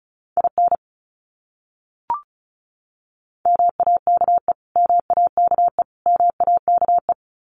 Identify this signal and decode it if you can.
Morse code, heard as a keyed tone, about 34 wpm, 700 Hz. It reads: IN MAKE MAKE MAKE
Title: Top 300 Words Repeated 35wpm